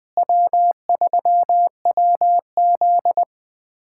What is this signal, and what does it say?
Morse code, heard as a keyed tone, about 20 wpm, 700 Hz. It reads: W3WZ
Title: USA Callsign DE Callsign 20wpm